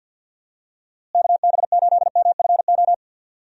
Morse code, heard as a keyed tone, about 48 wpm, 700 Hz. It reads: KB8MFO
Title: USA Call Signs - 6 Characters - Repeated 2x - 50wpm